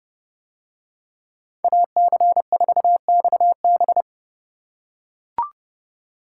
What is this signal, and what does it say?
Morse code, heard as a keyed tone, about 28 wpm, 700 Hz. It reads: AC4XB